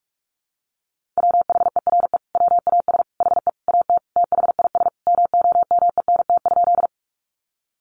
Morse code, heard as a keyed tone, about 44 wpm, 700 Hz. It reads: WHERE WAS HE AT THIS MOMENT?
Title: Sentences from Top 1000 Words 45wpm